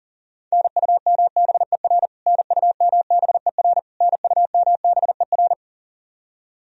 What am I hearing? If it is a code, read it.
Morse code, heard as a keyed tone, about 40 wpm, 700 Hz. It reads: NUMBER NUMBER NUMBER
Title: Top 100 Words Repeated 40wpm